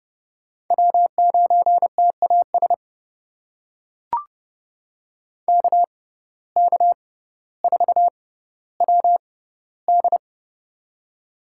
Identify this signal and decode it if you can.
Morse code, heard as a keyed tone, about 30 wpm, spaced as for 15 wpm, 700 Hz. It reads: W9TAS KK4WD
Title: USA Call Signs - 5 Characters - 6x Character Spacing - 30wpm